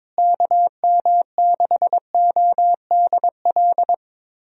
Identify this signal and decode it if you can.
Morse code, heard as a keyed tone, about 22 wpm, 700 Hz. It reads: KM6ODL
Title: USA Call Signs - 6 Characters - Repeated 2x - 22wpm